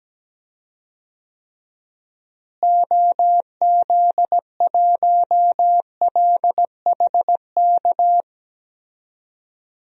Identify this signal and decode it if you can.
Morse code, heard as a keyed tone, about 17 wpm, 700 Hz. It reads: OZ1LHK
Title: DXCC-100 Call Signs - 6 Characters - Repeated 2x - 17wpm